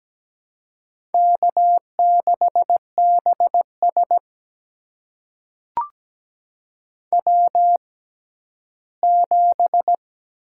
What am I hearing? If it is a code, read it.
Morse code, heard as a keyed tone, about 17 wpm, spaced as for 12 wpm, 700 Hz. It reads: K6BS W7
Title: USA Call Signs - 4 Characters - 6x Character Spacing - 17wpm